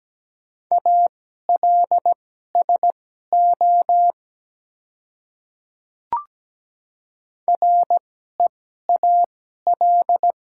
Morse code, heard as a keyed tone, about 17 wpm, spaced as for 12 wpm, 690 Hz. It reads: ALSO REAL
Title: 2x Character Spacing - Top 300 Words - 17wpm